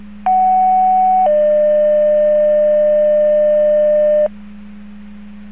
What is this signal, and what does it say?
Two-tone page, heard as a one-second tone, then a three-second tone, about 750 then 600 Hz, with noise. It is heard from 0.3 to 4.3 s.